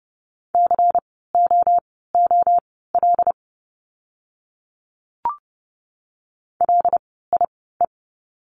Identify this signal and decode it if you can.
Morse code, heard as a keyed tone, about 30 wpm, spaced as for 17 wpm, 700 Hz. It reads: COOL LIE
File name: Lesson 16f - Letter c - Words 30wpm